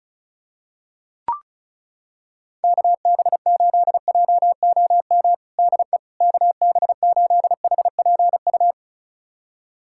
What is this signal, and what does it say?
Morse code, heard as a keyed tone, about 36 wpm, 700 Hz. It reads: KB8JOM DE KB8HPU
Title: USA Callsign DE Callsign - 6 Characters - 35wpm